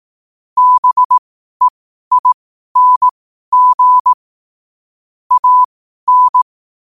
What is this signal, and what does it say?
Morse code, heard as a keyed tone, about 18 wpm, spaced as for 12 wpm, 1000 Hz. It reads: BEING AN
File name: qotdfiles / 12wpm